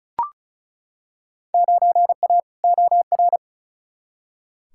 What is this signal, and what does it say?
Morse code, heard as a keyed tone, about 36 wpm, 700 Hz. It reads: 9A OR
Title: Field Day Exchanges Encore - 35wpm